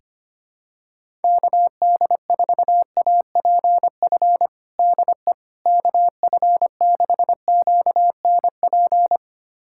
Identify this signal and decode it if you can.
Morse code, heard as a keyed tone, about 25 wpm, 700 Hz. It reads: KD4APF DE KF6QNP
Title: USA Callsign DE Callsign - 6 Characters - 25wpm